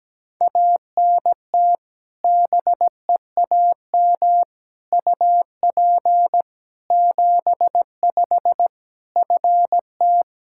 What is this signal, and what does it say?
Morse code, heard as a keyed tone, about 17 wpm, 700 Hz. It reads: ANT BEAM UP 75 FT
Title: Practice QSO - Exchange 2 - Info - 17wpm